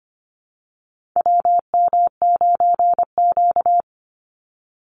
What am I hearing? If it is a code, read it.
Morse code, heard as a keyed tone, about 25 wpm, 700 Hz. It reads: WM9Q
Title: Call Signs Repeated 3x - 25wpm